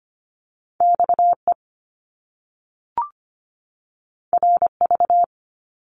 Morse code, heard as a keyed tone, about 25 wpm, 700 Hz. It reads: XE RV
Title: Top Two Letter Combinations in Words 25wpm